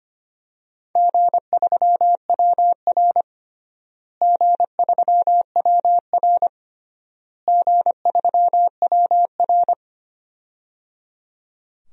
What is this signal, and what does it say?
Morse code, heard as a keyed tone, about 25 wpm, 700 Hz. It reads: G3WR G3WR G3WR
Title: DXCC-100 Call Signs - 4 Characters - Repeated 3x - 25wpm